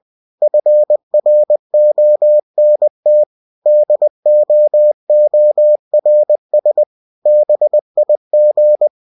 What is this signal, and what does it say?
Morse code, heard as a keyed tone, about 20 wpm, 600 Hz. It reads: FRONT DOORS BIG
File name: cw-books / winnie-the-pooh / winnie-20wpm-page-0017